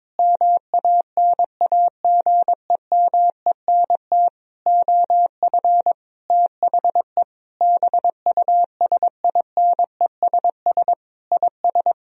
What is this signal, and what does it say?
Morse code, heard as a keyed tone, about 22 wpm, 700 Hz. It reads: MANAGEMENT OF THE BUSINESS IS
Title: Sentences from Top 1000 Words - Review Part 4 - 22wpm